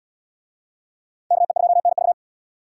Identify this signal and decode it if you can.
Morse code, heard as a keyed tone, about 78 wpm, 700 Hz. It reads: KE2NV